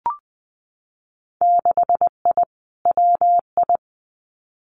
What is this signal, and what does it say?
Morse code, heard as a keyed tone, about 20 wpm, 700 Hz. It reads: 6I WI
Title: Winter Field Day Exchanges Encore 20wpm